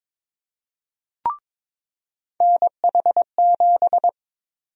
Morse code, heard as a keyed tone, about 21 wpm, 700 Hz. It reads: NH7
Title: USA Call Sign Prefixes - 22wpm